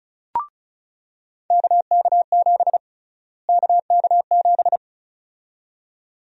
Morse code, heard as a keyed tone, about 34 wpm, 700 Hz. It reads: KK7 KK7
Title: USA Call Sign Prefixes - Repeated 2x - 35wpm